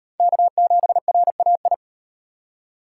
Morse code, heard as a keyed tone, about 35 wpm, 700 Hz. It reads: K7RAI